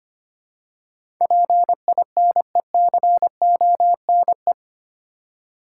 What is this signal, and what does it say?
Morse code, heard as a keyed tone, about 25 wpm, 700 Hz. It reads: PINECONE
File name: practice / 25wpm / Difficult Words with Hints - 25wpm